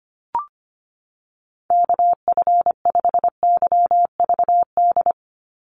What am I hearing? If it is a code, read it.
Morse code, heard as a keyed tone, about 25 wpm, 700 Hz. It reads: KF5YVD
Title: USA Call Signs 25wpm